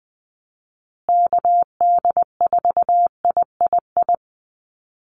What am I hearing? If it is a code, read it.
Morse code, heard as a keyed tone, about 20 wpm, 700 Hz. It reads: KD4III
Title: USA Call Signs Repeated 2x 20wpm